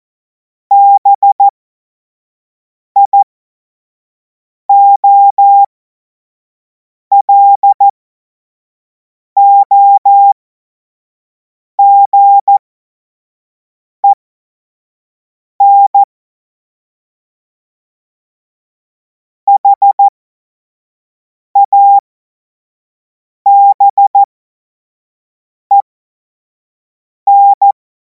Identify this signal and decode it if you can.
Morse code, heard as a keyed tone, about 14 wpm, spaced as for 5 wpm, 800 Hz. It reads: BIOLOGEN HABEN
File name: Morsekurs / Witze / Witz 24. Februar 2026 25 F70